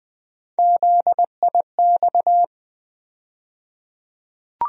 Morse code, heard as a keyed tone, about 20 wpm, 700 Hz. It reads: ZIX
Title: USA Call Sign Postfixes - 3 Letters 20wpm